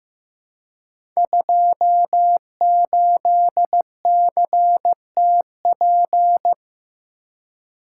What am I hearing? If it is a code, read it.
Morse code, heard as a keyed tone, about 15 wpm, 700 Hz. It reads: 28CTP
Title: Code Groups - 5 Letter:Numbers - 15wpm